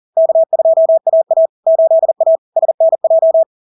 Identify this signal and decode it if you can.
Morse code, heard as a keyed tone, about 40 wpm, 650 Hz. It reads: K2AA 8A SNJ
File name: LARK-FD-2022-Log-in-CW-40wpm-650-Hz